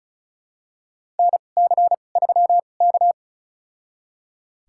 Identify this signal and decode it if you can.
Morse code, heard as a keyed tone, about 36 wpm, spaced as for 26 wpm, 700 Hz. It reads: NC3K
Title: USA Call Signs - 4 Characters - 2x Character Spacing - 35wpm